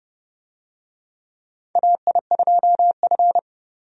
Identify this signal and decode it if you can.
Morse code, heard as a keyed tone, about 32 wpm, 700 Hz. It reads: AI2F
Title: USA Call Signs - 4 Characters - 30wpm